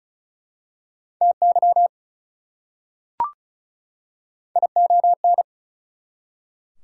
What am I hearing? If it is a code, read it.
Morse code, heard as a keyed tone, about 36 wpm, 690 Hz. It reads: TY ION
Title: Prefixes and Suffixes 35wpm